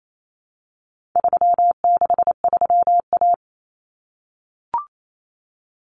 Morse code, heard as a keyed tone, about 28 wpm, 700 Hz. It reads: 363A
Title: Common Cut Numbers - 4-Digit 28wpm